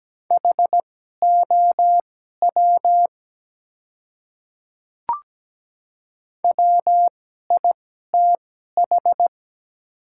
Morse code, heard as a keyed tone, about 17 wpm, spaced as for 12 wpm, 700 Hz. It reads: HOW WITH
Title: 2x Character Spacing - Top 300 Words - 17wpm